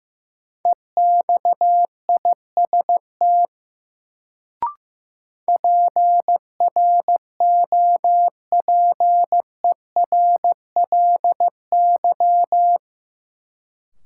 Morse code, heard as a keyed tone, about 15 wpm, 700 Hz. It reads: EXIST PROPERLY
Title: Sentences from Top 1100 Words - New Word Drill Alt - 15wpm